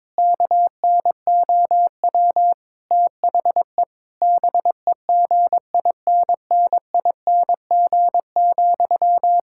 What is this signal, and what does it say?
Morse code, heard as a keyed tone, about 22 wpm, 700 Hz. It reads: KNOW THE BEGINNING,
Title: Sentences from Top 1000 Words - Review Part 6 - 22wpm